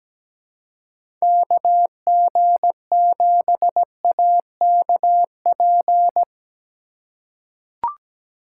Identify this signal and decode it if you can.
Morse code, heard as a keyed tone, about 17 wpm, 700 Hz. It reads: KG7AKP
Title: USA Call Signs - 6 Characters - 3x Character Spacing - 17wpm